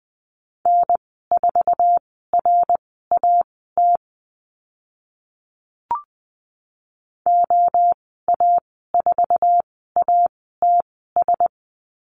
Morse code, heard as a keyed tone, about 20 wpm, spaced as for 15 wpm, 700 Hz. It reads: N4RAT OA4ATS
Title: 055 Lesson 10g - Letter r - Call Signs 20wpm